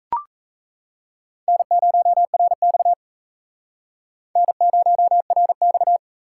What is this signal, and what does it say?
Morse code, heard as a keyed tone, about 37 wpm, 700 Hz. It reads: N0RX N0RX